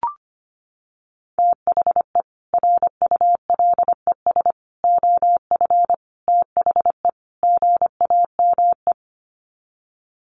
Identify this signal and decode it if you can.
Morse code, heard as a keyed tone, about 25 wpm, 700 Hz. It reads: THE RULES OF THE GAME
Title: Sets of 5 Words Encore2 25wpm